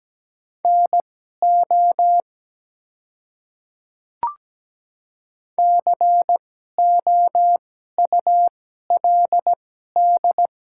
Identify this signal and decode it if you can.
Morse code, heard as a keyed tone, about 17 wpm, spaced as for 12 wpm, 700 Hz. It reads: NO COULD
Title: 2x Character Spacing - Top 100 Words - 17wpm